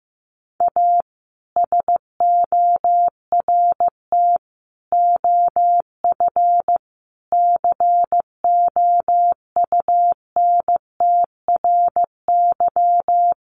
Morse code, heard as a keyed tone, about 14 wpm, 700 Hz. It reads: A SORT OF COUNTRY
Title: Sentences from Top 1300 Words 15wpm